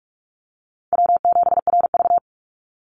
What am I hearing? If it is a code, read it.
Morse code, heard as a keyed tone, about 44 wpm, 690 Hz. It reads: W7RV
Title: NA SOTA Chaser Callsigns 45wpm